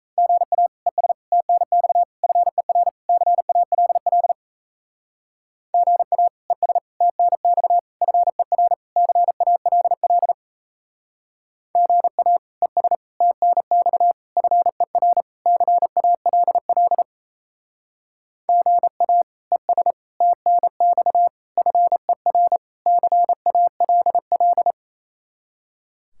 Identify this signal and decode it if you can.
Morse code, heard as a keyed tone, about 36 wpm, 700 Hz. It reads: GA ES TNX FER CALL GA ES TNX FER CALL GA ES TNX FER CALL GA ES TNX FER CALL